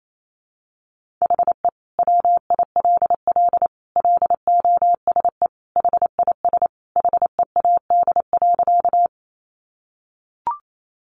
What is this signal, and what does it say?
Morse code, heard as a keyed tone, about 28 wpm, 700 Hz. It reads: HE WILL LOSE HIS HEAD.
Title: Sentences from Top 1100 Words 28wpm